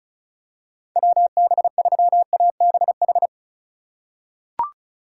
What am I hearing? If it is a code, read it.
Morse code, heard as a keyed tone, about 34 wpm, 700 Hz. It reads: WB3ABH
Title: USA Call Signs - 6 Characters - 4x Character Spacing - 35wpm